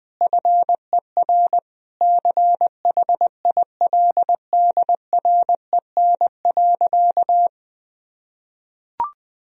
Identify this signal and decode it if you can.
Morse code, heard as a keyed tone, about 20 wpm, 700 Hz. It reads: FER CHILDREN.
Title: Sentences from Top 400 Words and Abbreviations 20wpm